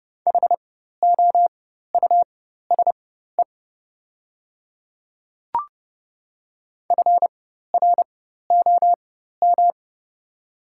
Morse code, heard as a keyed tone, about 30 wpm, spaced as for 14 wpm, 700 Hz. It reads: HOUSE FROM
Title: 4x Character Spacing - Top 200 Words - 30wpm